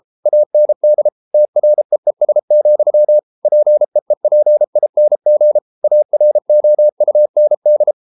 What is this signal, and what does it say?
Morse code, heard as a keyed tone, about 33 wpm, 600 Hz. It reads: AND TREES, PEEPING AROUND